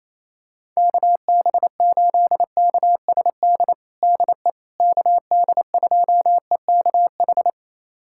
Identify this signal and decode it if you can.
Morse code, heard as a keyed tone, about 28 wpm, 700 Hz. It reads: KB8KSD DE KD2EKH
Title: USA Callsign DE Callsign 28wpm